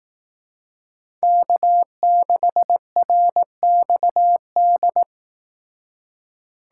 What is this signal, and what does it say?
Morse code, heard as a keyed tone, about 18 wpm, 700 Hz. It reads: K6RXD
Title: Speed Racing - USA 1x3 Call Signs 15wpm